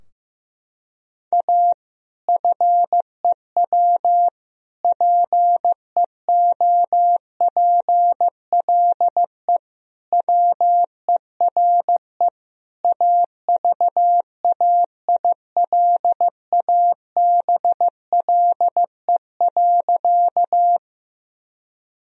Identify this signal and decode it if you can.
Morse code, heard as a keyed tone, about 15 wpm, 700 Hz. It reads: A FEW PEOPLE WERE AVAILABLE.
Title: Sentences from Top 1000 Words - Review Part 3 - 15wpm